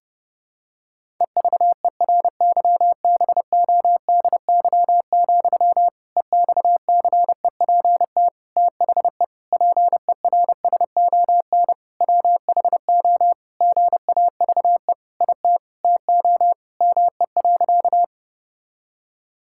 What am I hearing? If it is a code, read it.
Morse code, heard as a keyed tone, about 32 wpm, 700 Hz. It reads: EVERYBODY, EXCEPT THE PERSON WHO GAVE IT TO ME.
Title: Sentences from Top 1600 Words 30wpm